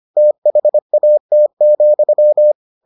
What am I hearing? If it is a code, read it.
Morse code, heard as a keyed tone, about 25 wpm, 600 Hz. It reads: THAT,